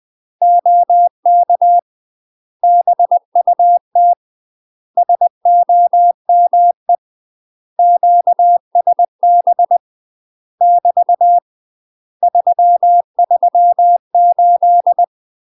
Morse code, heard as a keyed tone, about 20 wpm, 700 Hz. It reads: OK BUT SOME QSB <BT> 338
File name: UK MASTER QSOs TRAINING / 20WPM / UK MASTER QSOs 0013